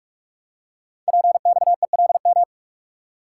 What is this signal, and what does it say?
Morse code, heard as a keyed tone, about 44 wpm, 690 Hz. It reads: PXELM
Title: Code Groups - 5 Letters - 45wpm